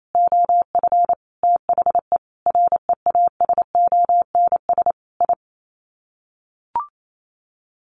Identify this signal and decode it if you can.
Morse code, heard as a keyed tone, about 28 wpm, 700 Hz. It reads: OF THE REASONS I
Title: Sets of 5 Words Encore2 28wpm